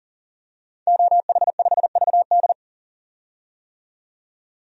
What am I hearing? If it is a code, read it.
Morse code, heard as a keyed tone, about 40 wpm, 700 Hz. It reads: OH5VD
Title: DXCC-100 Call Signs - 5 Characters - 40wpm